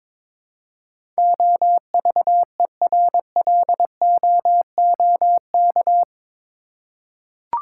Morse code, heard as a keyed tone, about 22 wpm, 700 Hz. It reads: OVERLOOK
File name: practice / 22wpm / Eight Letter Words - 22wpm